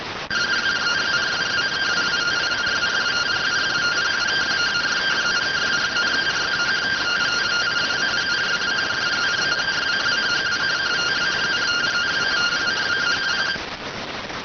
Two-tone teleprinter signal, about 170 Hz shift.